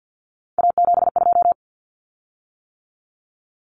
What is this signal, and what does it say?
Morse code, heard as a keyed tone, about 50 wpm, 700 Hz. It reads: A72
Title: Common Cut Numbers - 3-Digit 50wpm